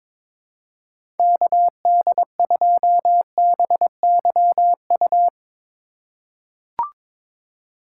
Morse code, heard as a keyed tone, about 22 wpm, 700 Hz. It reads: KD2BYU
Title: USA Call Signs - 6 Characters - 2x Character Spacing - 22wpm